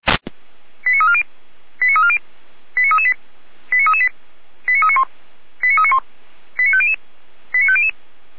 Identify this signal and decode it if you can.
Five-tone selective call (ZVEI), heard as quick runs of five tones, tones of about 80 ms. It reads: 89340 89340 89338 89338 89391 89391 89500 89500